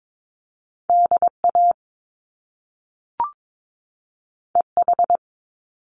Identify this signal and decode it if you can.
Morse code, heard as a keyed tone, about 22 wpm, 700 Hz. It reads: DA EH
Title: Top Two Letter Combinations in Words 22wpm